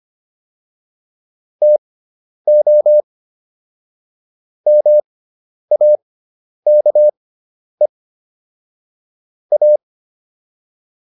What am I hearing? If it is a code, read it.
Morse code, heard as a keyed tone, about 25 wpm, spaced as for 10 wpm, 600 Hz. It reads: TO MAKE A